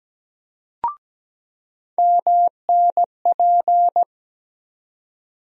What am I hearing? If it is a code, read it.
Morse code, heard as a keyed tone, about 17 wpm, 700 Hz. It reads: MNP
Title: USA Call Sign Postfixes - 3 Letters 17wpm